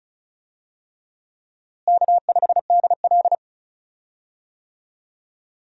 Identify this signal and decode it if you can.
Morse code, heard as a keyed tone, about 36 wpm, 700 Hz. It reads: K5DL
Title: USA Call Signs - 4 Characters - Repeated 3x - 35wpm